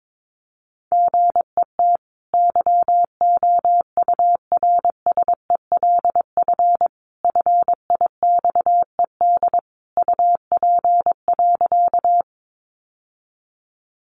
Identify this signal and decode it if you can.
Morse code, heard as a keyed tone, about 22 wpm, 700 Hz. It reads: GET YOURSELF FIXED UP.
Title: Sentences from Top 1100 Words 22wpm